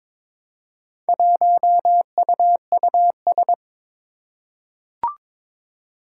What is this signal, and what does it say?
Morse code, heard as a keyed tone, about 22 wpm, 700 Hz. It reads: 1UUS
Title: Four Challenging Letter-Number Combinations 22wpm